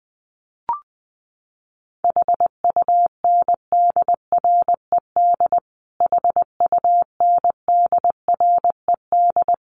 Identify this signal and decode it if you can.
Morse code, heard as a keyed tone, about 20 wpm, 700 Hz. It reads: HUNDRED HUNDRED
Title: Top 500 Words Repeated 20wpm